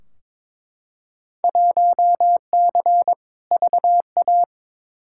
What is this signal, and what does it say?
Morse code, heard as a keyed tone, about 22 wpm, 700 Hz. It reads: 1C VA
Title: Field Day Exchanges Encore - 22wpm